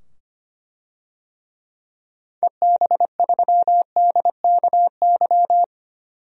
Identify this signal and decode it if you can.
Morse code, heard as a keyed tone, about 25 wpm, 700 Hz. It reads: EB3DKY
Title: DXCC-100 Call Signs - 6 Characters - 25wpm